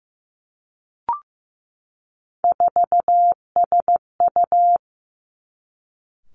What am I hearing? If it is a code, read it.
Morse code, heard as a keyed tone, about 15 wpm, 700 Hz. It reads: 4SU
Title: Three Challenging Letter-Number Combinations 15wpm